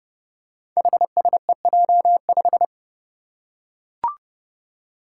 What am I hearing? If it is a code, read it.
Morse code, heard as a keyed tone, about 30 wpm, 700 Hz. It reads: HSEJ5
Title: Five Challenging Letter-Number Combinations 30wpm